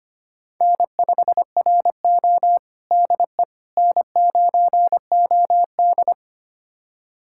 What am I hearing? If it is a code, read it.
Morse code, heard as a keyed tone, about 25 wpm, 700 Hz. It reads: N5RO DE N9OD
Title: USA Callsign DE Callsign - 4 Characters - 25wpm